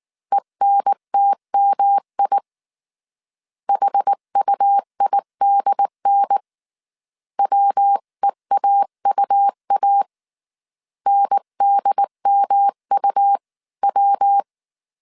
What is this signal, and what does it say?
Morse code, heard as a keyed tone, about 19 wpm, 800 Hz. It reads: ENTMI HUIDN WEAUA NDMUW